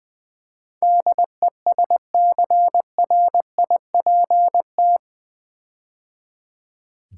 Morse code, heard as a keyed tone, about 20 wpm, 700 Hz. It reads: DESCRIPT
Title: Build A Word - 4 Letters At A Time - 11-Letter Words - 20wpm